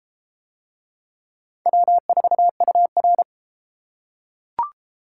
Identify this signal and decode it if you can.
Morse code, heard as a keyed tone, about 30 wpm, 700 Hz. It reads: W4UR